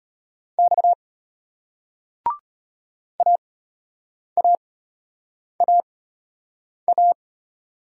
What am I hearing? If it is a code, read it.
Morse code, heard as a keyed tone, about 34 wpm, spaced as for 8 wpm, 700 Hz. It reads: X AAAA